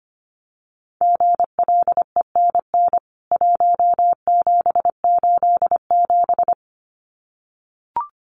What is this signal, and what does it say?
Morse code, heard as a keyed tone, about 25 wpm, 700 Hz. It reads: GLENN 1787
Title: CWT Exchange 25wpm